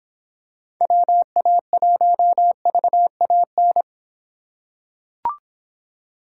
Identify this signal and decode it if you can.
Morse code, heard as a keyed tone, about 26 wpm, 700 Hz. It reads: WA1VAN